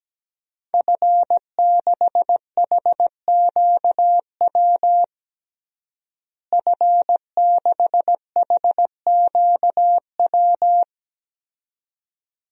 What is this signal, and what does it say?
Morse code, heard as a keyed tone, about 17 wpm, 700 Hz. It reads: F6HQW F6HQW
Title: DXCC-100 Call Signs - 5 Characters - Repeated 2x - 17wpm